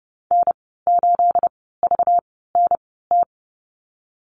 Morse code, heard as a keyed tone, about 30 wpm, spaced as for 17 wpm, 700 Hz. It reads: N8VNT
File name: Lesson 37g - Letter x - Call Signs 30wpm